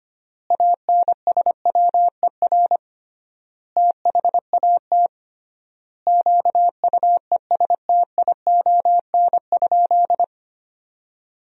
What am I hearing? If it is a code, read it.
Morse code, heard as a keyed tone, about 25 wpm, 700 Hz. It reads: ANSWER THAT QUESTION?
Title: 3x Word-Spacing - Sentences from Top 400 Words - 25wpm